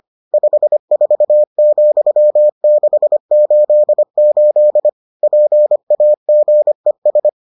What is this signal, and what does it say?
Morse code, heard as a keyed tone, about 25 wpm, 600 Hz. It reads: 54,688 PAGES